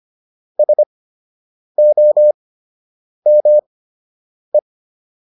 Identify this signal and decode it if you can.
Morse code, heard as a keyed tone, about 25 wpm, spaced as for 8 wpm, 600 Hz. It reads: SOME